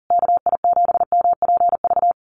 Morse code, heard as a keyed tone, about 40 wpm, 700 Hz. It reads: KI7MPV
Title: USA Call Signs Repeated 3x 40wpm